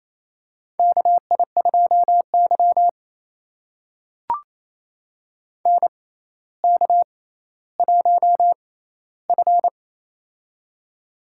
Morse code, heard as a keyed tone, about 28 wpm, spaced as for 14 wpm, 700 Hz. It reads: KI2Y NK1F
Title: USA Call Signs - 4 Characters - 6x Character Spacing - 28wpm